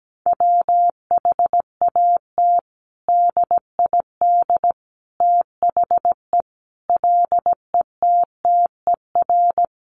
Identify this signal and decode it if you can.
Morse code, heard as a keyed tone, about 17 wpm, 700 Hz. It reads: WHAT DID THE LETTER
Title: Sentences from Top 500 Words 17wpm